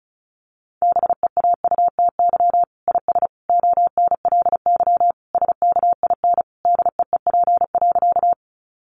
Morse code, heard as a keyed tone, about 34 wpm, 700 Hz. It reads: BEAUTY IS ONLY SKIN DEEP.
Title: Sentences from Top 1300 Words 35wpm